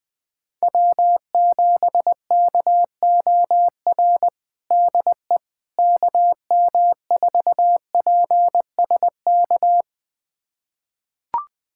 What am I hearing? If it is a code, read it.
Morse code, heard as a keyed tone, about 20 wpm, 700 Hz. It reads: W7KOR DE KM4PSK
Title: USA Callsign DE Callsign 20wpm